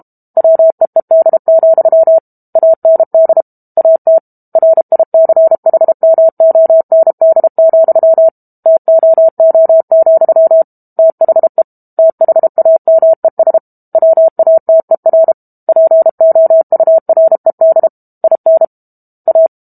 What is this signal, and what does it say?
Morse code, heard as a keyed tone, about 32 wpm, 650 Hz. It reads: WEED, AND AT RICHMOND, TOO, THE THAMES WATER POURED IN A
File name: Book Two / 06-Chapter 6 - 32 WPM